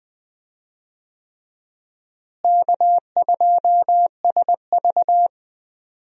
Morse code, heard as a keyed tone, about 20 wpm, 700 Hz. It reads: K2SV
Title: USA Call Signs - 4 Characters - Repeated 3x - 20wpm